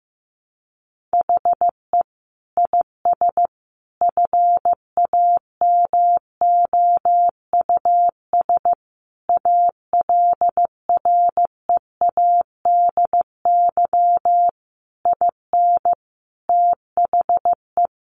Morse code, heard as a keyed tone, about 15 wpm, 700 Hz. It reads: HE IS FAMOUS ALREADY IN THE
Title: Sentences from Top 1500 Words 15wpm